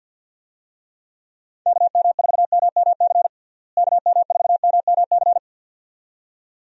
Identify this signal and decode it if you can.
Morse code, heard as a keyed tone, about 50 wpm, 700 Hz. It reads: KM4MMC KM4MMC
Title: USA Call Signs - 6 Characters - Repeated 2x - 50wpm